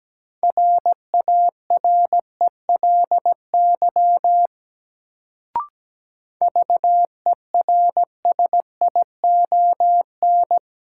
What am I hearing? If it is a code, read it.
Morse code, heard as a keyed tone, about 17 wpm, 700 Hz. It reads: RARELY VERSION
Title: Sentences from Top 1400 Words - New Word Drill - 17wpm